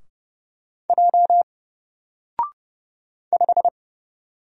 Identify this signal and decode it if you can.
Morse code, heard as a keyed tone, about 30 wpm, 700 Hz. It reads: J 5
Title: Single Letter-Number Alt 30wpm